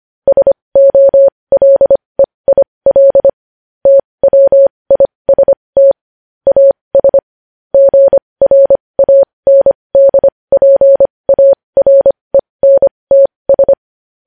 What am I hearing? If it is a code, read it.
Morse code, heard as a keyed tone, about 25 wpm, spaced as for 20 wpm, 550 Hz. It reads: SOLEIL TWIST AS GRANDPARENTS
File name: podcast / News headlines-1x-25x20